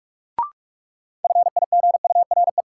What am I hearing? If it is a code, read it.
Morse code, heard as a keyed tone, about 42 wpm, 700 Hz. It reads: FIGURE
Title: Sets of 4 Words - New Word Drill - 45wpm